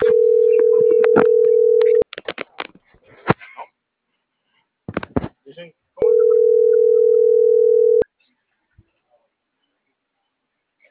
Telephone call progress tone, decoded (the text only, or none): ringback tone